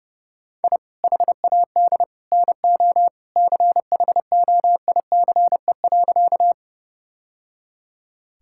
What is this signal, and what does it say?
Morse code, heard as a keyed tone, about 30 wpm, 700 Hz. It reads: I HAD NO CHOICE.
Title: Sentences from Top 1000 Words - Review Part 6 - 30wpm